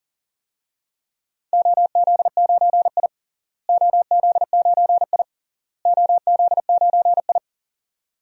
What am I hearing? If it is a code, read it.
Morse code, heard as a keyed tone, about 40 wpm, 700 Hz. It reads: OZ9I OZ9I OZ9I
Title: DXCC-100 Call Signs - 4 Characters - Repeated 3x - 40wpm